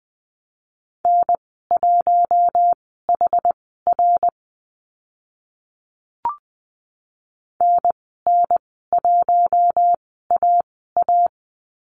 Morse code, heard as a keyed tone, about 20 wpm, spaced as for 15 wpm, 700 Hz. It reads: N1HR NN1AA
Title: 062 Lesson 11g - Letter h - Call Signs 20wpm